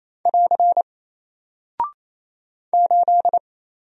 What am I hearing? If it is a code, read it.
Morse code, heard as a keyed tone, about 28 wpm, 700 Hz. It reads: <AR> 8
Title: Single Character (Letter:Number:Punctuation:Prosign) Alt 28wpm